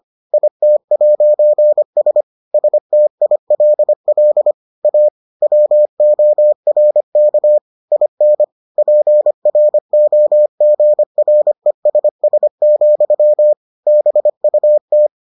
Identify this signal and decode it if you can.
Morse code, heard as a keyed tone, about 25 wpm, 600 Hz. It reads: IT'S STILL A WORK IN PROGRESS, BUT